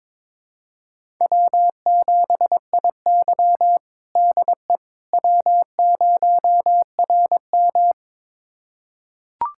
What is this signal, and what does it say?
Morse code, heard as a keyed tone, about 22 wpm, 700 Hz. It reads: W7IY DE W0RM
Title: USA Callsign DE Callsign - 4 Characters - 22wpm